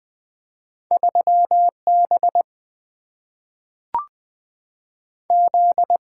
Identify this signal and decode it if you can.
Morse code, heard as a keyed tone, about 20 wpm, 700 Hz. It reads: 3 B Z
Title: Single-Letter Single-Number 20wpm